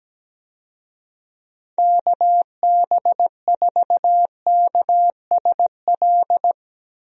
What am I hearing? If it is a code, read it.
Morse code, heard as a keyed tone, about 17 wpm, 700 Hz. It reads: KB4KSL
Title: USA Call Signs - 6 Characters - Repeated 3x - 17wpm